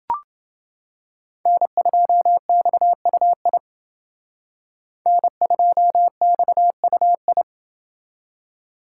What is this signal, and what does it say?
Morse code, heard as a keyed tone, about 28 wpm, 700 Hz. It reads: N2XUI N2XUI